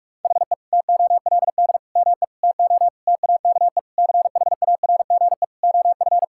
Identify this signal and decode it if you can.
Morse code, heard as a keyed tone, about 46 wpm, 700 Hz. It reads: HE TOLD ME TO TAKE CHARGE OF
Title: Sentences from Top 1000 Words - Review Part 6 - 45wpm